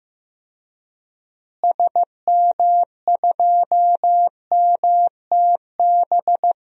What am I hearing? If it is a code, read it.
Morse code, heard as a keyed tone, about 15 wpm, 700 Hz. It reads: SM2MTB
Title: DXCC-100 Call Signs - 6 Characters - Repeated 2x - 15wpm